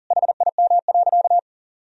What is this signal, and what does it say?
Morse code, heard as a keyed tone, about 40 wpm, 700 Hz. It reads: HIM.
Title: Sentences from Top 1000 Words - Review Part 1 - 40wpm